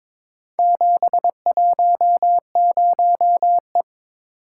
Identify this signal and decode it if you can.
Morse code, heard as a keyed tone, about 22 wpm, 700 Hz. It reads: 710E
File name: practice / 22wpm / Four Letter-Number Combinations - 22wpm